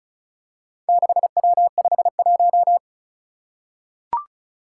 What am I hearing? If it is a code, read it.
Morse code, heard as a keyed tone, about 36 wpm, 700 Hz. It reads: 6W51
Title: Four Challenging Letter-Number Combinations 35wpm